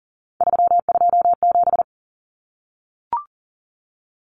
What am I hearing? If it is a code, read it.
Morse code, heard as a keyed tone, about 40 wpm, 700 Hz. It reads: 327
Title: Common Cut Numbers - 3-Digit 40wpm